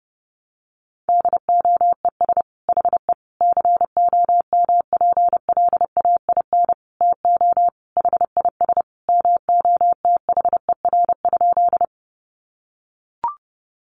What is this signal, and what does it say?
Morse code, heard as a keyed tone, about 30 wpm, 700 Hz. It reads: DOES HE COMPLAIN TO HIS MOTHER?
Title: Sentences from Top 1300 Words 30wpm